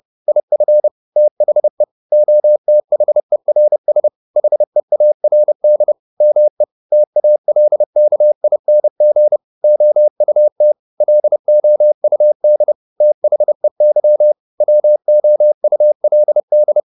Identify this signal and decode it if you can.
Morse code, heard as a keyed tone, about 30 wpm, 600 Hz. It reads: IF THE OTHERS HEARD ME TALKING OUT LOUD THEY WOULD
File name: the old man and the sea / the old man and the sea 1 30wpm